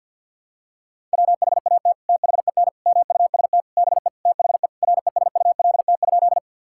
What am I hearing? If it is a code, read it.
Morse code, heard as a keyed tone, about 52 wpm, 700 Hz. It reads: WHAT THEN MUST BE THE RESULT?
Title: Sentences from Top 1000 Words - Review Part 5 - 50wpm